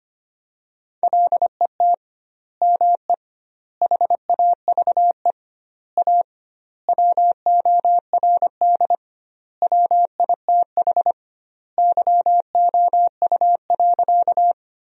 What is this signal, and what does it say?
Morse code, heard as a keyed tone, about 25 wpm, 700 Hz. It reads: LET ME HAVE A WORD WITH YOU.
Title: 2x Word-Spacing - Sentences from Top 300 Words - 25wpm